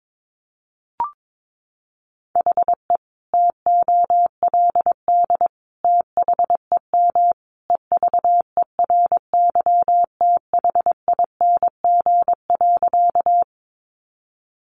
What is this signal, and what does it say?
Morse code, heard as a keyed tone, about 22 wpm, 700 Hz. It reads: HE TOLD THEM EVERYTHING.
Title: Sentences from Top 700 Words 22wpm